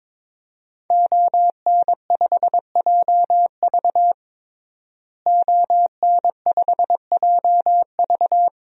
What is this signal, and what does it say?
Morse code, heard as a keyed tone, about 22 wpm, 700 Hz. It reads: ON5JV ON5JV
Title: DXCC-100 Call Signs - 5 Characters - Repeated 2x - 22wpm